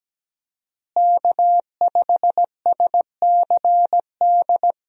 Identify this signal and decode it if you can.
Morse code, heard as a keyed tone, about 17 wpm, 700 Hz. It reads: K5SCD
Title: USA Call Signs - 5 Characters - Repeated 2x - 17wpm